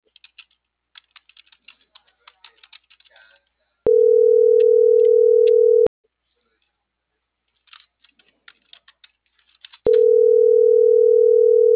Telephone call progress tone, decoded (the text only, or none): ringback tone